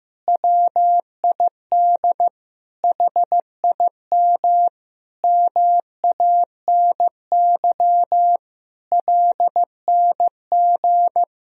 Morse code, heard as a keyed tone, about 15 wpm, 700 Hz. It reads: WID HIM MANY LNG
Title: Sentences from Top 200 Words and Abbreviations 15wpm